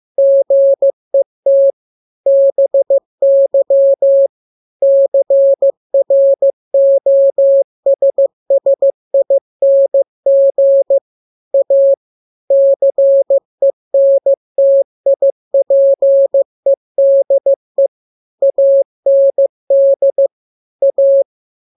Morse code, heard as a keyed tone, about 15 wpm, 550 Hz. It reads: GET BY CROSSING A CENTIPEDE AND A
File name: jokes / joke 12 15 0000